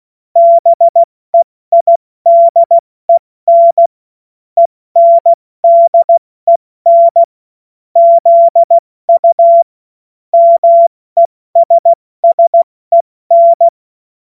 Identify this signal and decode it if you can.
Morse code, heard as a keyed tone, about 16 wpm, spaced as for 14 wpm, 680 Hz. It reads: BEIDEN ENDEN ZU MESSEN